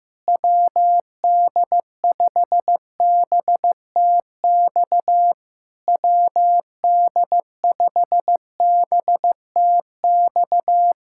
Morse code, heard as a keyed tone, about 15 wpm, 700 Hz. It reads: WD5BTX WD5BTX
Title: Speed-Racing Common QSO Phrases Encore - 15wpm